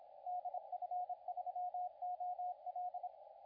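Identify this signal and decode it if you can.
Morse code, heard as a keyed tone, about 26 wpm, 720 Hz, with radio noise. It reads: DF3OL
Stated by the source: CW contest audio recorded at LA4C during the SAC CW contest.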